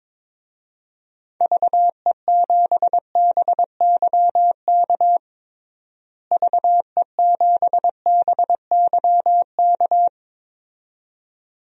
Morse code, heard as a keyed tone, about 22 wpm, 700 Hz. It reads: VE7BYK VE7BYK
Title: DXCC-100 Call Signs - 6 Characters - Repeated 2x - 22wpm